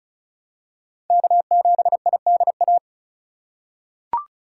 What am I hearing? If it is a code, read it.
Morse code, heard as a keyed tone, about 36 wpm, 700 Hz. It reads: K7IDA
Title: USA Call Signs - 3x Character Spacing - 35wpm